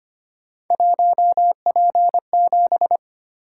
Morse code, heard as a keyed tone, about 25 wpm, 700 Hz. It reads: 1P7
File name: Three Letter-Number Combinations - 25wpm